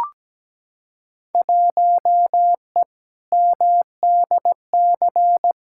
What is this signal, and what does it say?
Morse code, heard as a keyed tone, about 17 wpm, 700 Hz. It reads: 1E MDC
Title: Field Day Exchanges - 17wpm